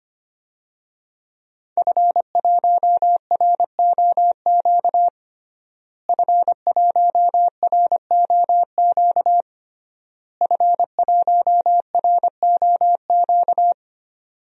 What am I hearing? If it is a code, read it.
Morse code, heard as a keyed tone, about 25 wpm, 700 Hz. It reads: F1ROQ F1ROQ F1ROQ
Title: DXCC-100 Call Signs - 5 Characters - Repeated 3x - 25wpm